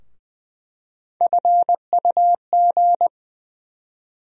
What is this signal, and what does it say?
Morse code, heard as a keyed tone, about 20 wpm, 700 Hz. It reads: FUG
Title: USA Call Sign Postfixes - 3 Letters 20wpm